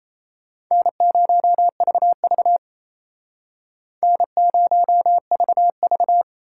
Morse code, heard as a keyed tone, about 36 wpm, 700 Hz. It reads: N0VV N0VV